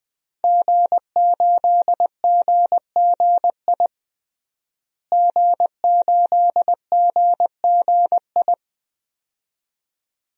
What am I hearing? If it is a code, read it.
Morse code, heard as a keyed tone, about 20 wpm, 700 Hz. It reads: G8GGI G8GGI
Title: DXCC-100 Call Signs - 5 Characters - Repeated 2x - 20wpm